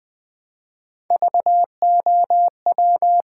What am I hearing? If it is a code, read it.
Morse code, heard as a keyed tone, about 20 wpm, 700 Hz. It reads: VOW
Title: Build A Word - 1 Letter At A Time - 3-Letter Words - 20wpm